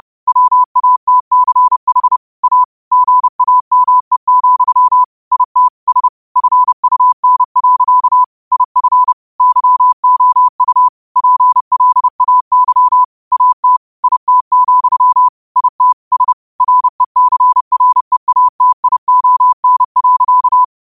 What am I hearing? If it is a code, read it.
Morse code, heard as a keyed tone, about 30 wpm, 1000 Hz. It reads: WATCH A GAME, IT S FUN. IF YOU PLAY AT IT, IT S RECREATION.
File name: qotdfiles / 30wpm